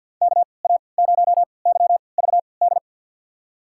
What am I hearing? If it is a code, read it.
Morse code, heard as a keyed tone, about 47 wpm, spaced as for 28 wpm, 700 Hz. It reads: KA0YVD
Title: USA Call Signs - 3x Character Spacing - 50wpm